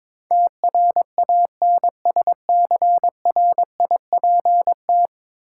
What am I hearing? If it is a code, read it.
Morse code, heard as a keyed tone, about 22 wpm, 700 Hz. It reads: TRANSCRIPT
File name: practice / 22wpm / 10 Letter Words - First 3 Letters Given - 22wpm